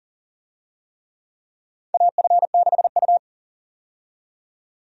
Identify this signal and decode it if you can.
Morse code, heard as a keyed tone, about 40 wpm, 700 Hz. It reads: AF6U
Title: USA Call Signs - 4 Characters - 40wpm